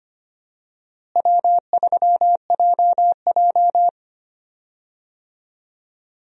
Morse code, heard as a keyed tone, about 25 wpm, 700 Hz. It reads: W3JJ
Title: USA Call Signs - 4 Characters - Repeated 3x - 25wpm